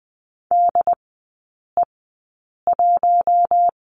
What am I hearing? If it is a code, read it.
Morse code, heard as a keyed tone, about 20 wpm, spaced as for 8 wpm, 700 Hz. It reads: DE1
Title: 067 Lesson 12e - Letter d - Three Characters 20wpm